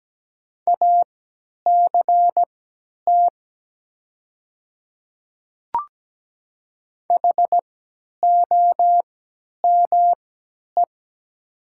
Morse code, heard as a keyed tone, about 17 wpm, spaced as for 10 wpm, 700 Hz. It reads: ACT HOME
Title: 3x Character Spacing - Top 200 Words - 17wpm